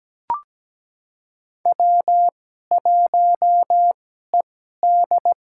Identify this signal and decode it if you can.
Morse code, heard as a keyed tone, about 17 wpm, spaced as for 12 wpm, 700 Hz. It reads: W1ED
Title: USA Call Signs - 4 Characters - 2x Character Spacing - 17wpm